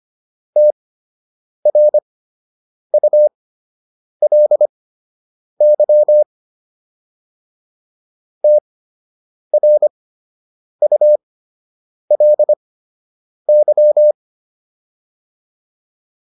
Morse code, heard as a keyed tone, about 25 wpm, spaced as for 8 wpm, 600 Hz. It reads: TRULY TRULY